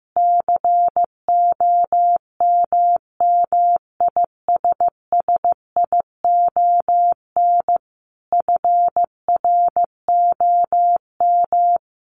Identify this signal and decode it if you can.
Morse code, heard as a keyed tone, about 15 wpm, 700 Hz. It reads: COMMISSION FROM
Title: Sentences from Top 1400 Words 15wpm